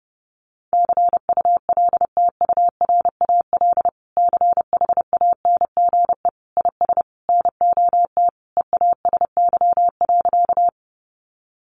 Morse code, heard as a keyed tone, about 28 wpm, 700 Hz. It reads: CULTURAL CHANGE IS NOT EASY.
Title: Sentences from Top 1500 Words 30wpm